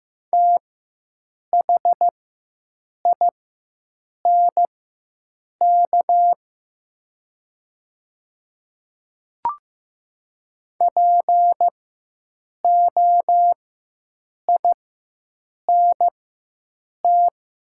Morse code, heard as a keyed tone, about 15 wpm, spaced as for 7 wpm, 700 Hz. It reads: THINK POINT
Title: 4x Character Spacing - Top 200 Words - 15wpm